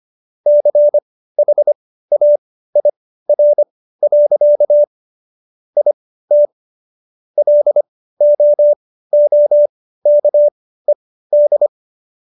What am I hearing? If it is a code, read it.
Morse code, heard as a keyed tone, about 25 wpm, spaced as for 15 wpm, 590 Hz. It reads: CHAIR. IT LOOKED